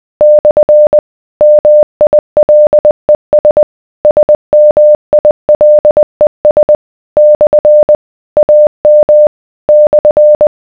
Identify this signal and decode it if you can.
Morse code, heard as a keyed tone, about 20 wpm, 600 Hz. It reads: / MILES SMILES / AM /